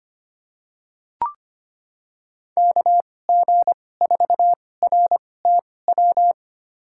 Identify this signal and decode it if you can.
Morse code, heard as a keyed tone, about 25 wpm, spaced as for 18 wpm, 700 Hz. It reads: KG4RTW
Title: USA Call Signs - 6 Characters - 2x Character Spacing - 25wpm